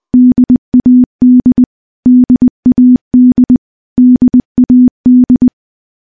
Morse code, heard as a keyed tone, about 20 wpm, 250 Hz. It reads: DAD DAD DAD
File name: morse-training-20WPM